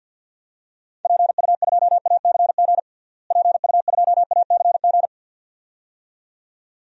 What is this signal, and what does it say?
Morse code, heard as a keyed tone, about 52 wpm, 700 Hz. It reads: PU2ACG PU2ACG
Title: DXCC-100 Call Signs - 6 Characters - Repeated 2x - 50wpm